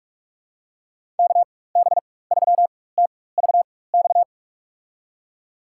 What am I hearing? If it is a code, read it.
Morse code, heard as a keyed tone, about 46 wpm, spaced as for 21 wpm, 700 Hz. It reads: KB3TVX
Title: USA Call Signs - 6 Characters - 4x Character Spacing - 45wpm